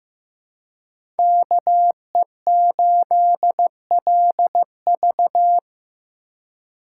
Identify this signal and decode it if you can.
Morse code, heard as a keyed tone, about 15 wpm, 700 Hz. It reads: KE8LV
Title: USA Call Signs - 5 Characters - 4x Character Spacing - 15wpm